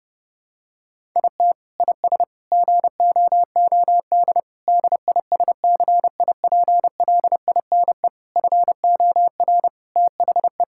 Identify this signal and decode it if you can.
Morse code, heard as a keyed tone, about 30 wpm, 700 Hz. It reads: IT IS GOOD DISCIPLINE FOR THE
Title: Sentences from Top 1000 Words - Review Part 6 - 30wpm